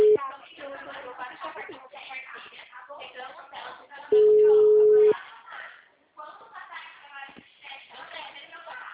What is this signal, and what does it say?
Ringback tone (Europe, 425 Hz), with speech.
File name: out- -2018- - - .22